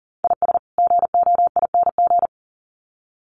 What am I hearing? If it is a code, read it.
Morse code, heard as a keyed tone, about 40 wpm, 700 Hz. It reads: IS GOING
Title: Sets of 2 Words 40wpm